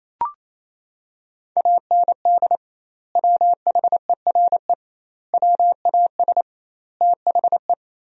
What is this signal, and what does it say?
Morse code, heard as a keyed tone, about 28 wpm, 700 Hz. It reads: AND WHERE WAS THE
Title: 2x Word-Spacing - Sentences from Top 200 Words - 28wpm